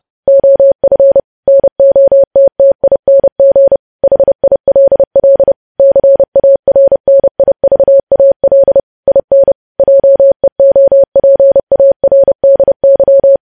Morse code, heard as a keyed tone, about 30 wpm, 550 Hz. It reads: OF NOTTING HILL CARNIVAL IN JEOPARDY
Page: news headlines in morse code at 30 wpm